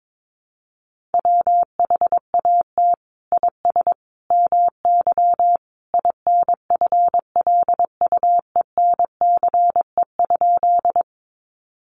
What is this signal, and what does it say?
Morse code, heard as a keyed tone, about 22 wpm, 700 Hz. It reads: WHAT IS MY INFLUENCE?
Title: Sentences from Top 1100 Words 22wpm